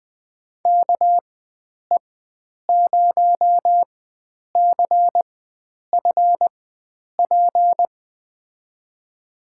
Morse code, heard as a keyed tone, about 20 wpm, spaced as for 9 wpm, 700 Hz. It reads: KE0CFP
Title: USA Call Signs - 6 Characters - 4x Character Spacing - 20wpm